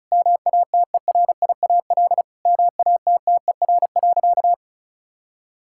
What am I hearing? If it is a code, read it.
Morse code, heard as a keyed tone, about 36 wpm, 700 Hz. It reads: MATERIAL MATTER.
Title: Sentences from Top 1000 Words - Review Part 5 - 35wpm